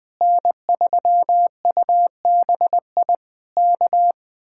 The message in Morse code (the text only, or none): N3UBI K